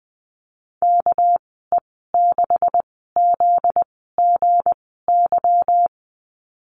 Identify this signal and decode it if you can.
Morse code, heard as a keyed tone, about 20 wpm, spaced as for 15 wpm, 700 Hz. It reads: KE6ZGY
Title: 245 Lesson 39g - Letter z - Call Signs 20wpm